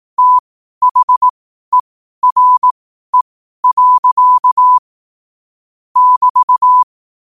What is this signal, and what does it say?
Morse code, heard as a keyed tone, about 18 wpm, spaced as for 12 wpm, 1000 Hz. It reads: THERE. <BT>
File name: qotdfiles / 12wpm